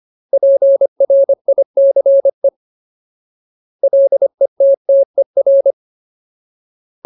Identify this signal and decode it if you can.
Morse code, heard as a keyed tone, about 25 wpm, 550 Hz. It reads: PRICE LETTER
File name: Ultimate Word List / 25WPM Ultimate